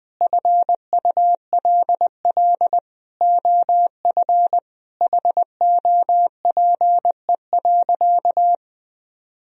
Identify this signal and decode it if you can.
Morse code, heard as a keyed tone, about 21 wpm, 700 Hz. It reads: FULL OF HOPE.
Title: Sentences from Top 1000 Words - Review Part 3 - 20wpm